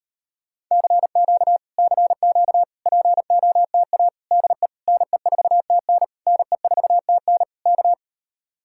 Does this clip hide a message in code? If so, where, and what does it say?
Yes: Morse code from 0.7 to 7.9 s, heard as a keyed tone, about 35 wpm, 700 Hz. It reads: CQ CQ POTA DE NE4TN NE4TN K